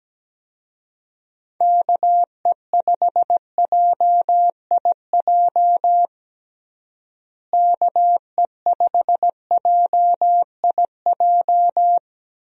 Morse code, heard as a keyed tone, about 17 wpm, 700 Hz. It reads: KE5JIJ KE5JIJ
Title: USA Call Signs - 6 Characters - Repeated 3x - 17wpm